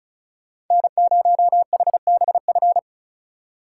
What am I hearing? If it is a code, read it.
Morse code, heard as a keyed tone, about 47 wpm, spaced as for 41 wpm, 700 Hz. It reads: N0HBF
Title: Speed Racing - USA 1x3 Call Signs 35wpm